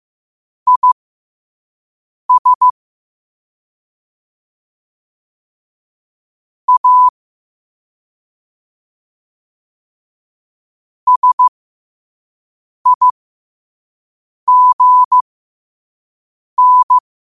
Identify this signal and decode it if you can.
Morse code, heard as a keyed tone, about 15 wpm, spaced as for 5 wpm, 1000 Hz. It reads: IS A SIGN
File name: qotdfiles / 5wpm